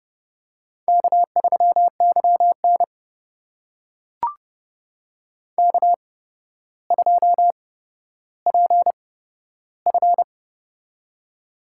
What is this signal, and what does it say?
Morse code, heard as a keyed tone, about 30 wpm, spaced as for 13 wpm, 700 Hz. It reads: K3YN K2PF
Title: USA Call Signs - 4 Characters - 8x Character Spacing - 30wpm